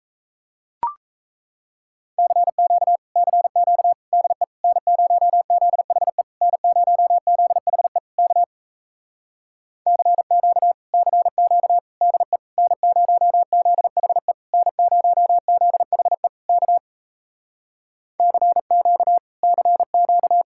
Morse code, heard as a keyed tone, about 36 wpm, 700 Hz. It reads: CQ CQ DE N0ZHE N0ZHE K CQ CQ DE N0ZHE N0ZHE K CQ CQ